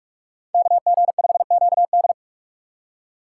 Morse code, heard as a keyed tone, about 49 wpm, 700 Hz. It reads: KG5QD